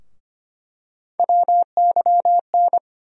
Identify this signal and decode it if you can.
Morse code, heard as a keyed tone, about 25 wpm, 700 Hz. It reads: WYN
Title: USA Call Sign Postfixes - 3 Letters 25wpm